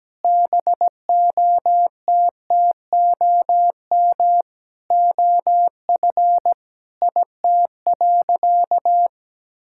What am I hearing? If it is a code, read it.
Morse code, heard as a keyed tone, about 17 wpm, 700 Hz. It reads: BOTTOM OF IT.
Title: Sentences from Top 1000 Words - Review Part 6 - 17wpm